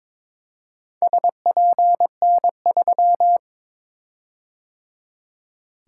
Morse code, heard as a keyed tone, about 22 wpm, 700 Hz. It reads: SPN3
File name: practice / 22wpm / Four Letter-Number Combinations - 22wpm